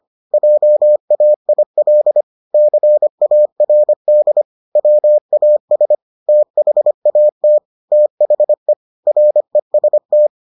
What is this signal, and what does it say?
Morse code, heard as a keyed tone, about 25 wpm, 600 Hz. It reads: JAIL CARD WAS THAT THE REST